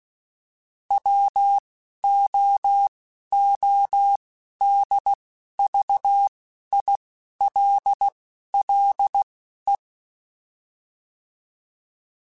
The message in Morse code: WOODVILLE

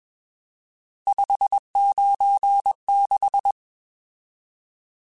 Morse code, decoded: 596

